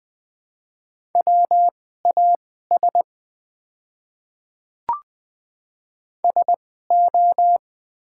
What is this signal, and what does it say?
Morse code, heard as a keyed tone, about 20 wpm, spaced as for 14 wpm, 700 Hz. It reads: WAS SO
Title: 2x Character Spacing - Top 500 Words - 20wpm